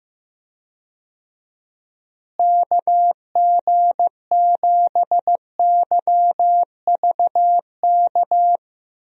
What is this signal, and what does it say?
Morse code, heard as a keyed tone, about 15 wpm, 700 Hz. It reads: KG7YVK
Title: USA Call Signs - 6 Characters - Repeated 2x - 15wpm